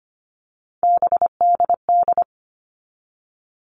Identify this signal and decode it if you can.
Morse code, heard as a keyed tone, about 25 wpm, 700 Hz. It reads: BDD
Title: All Cut Numbers - 3-Digit 25wpm